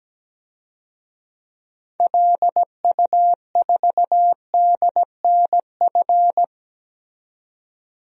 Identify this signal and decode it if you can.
Morse code, heard as a keyed tone, about 17 wpm, 700 Hz. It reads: LU4DNF
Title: DXCC-100 Call Signs - 6 Characters - 17wpm